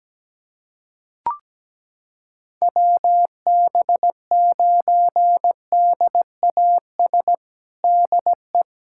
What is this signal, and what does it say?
Morse code, heard as a keyed tone, about 17 wpm, 700 Hz. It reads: WB9DAS DE
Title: USA Callsign DE Callsign - 6 Characters - 17wpm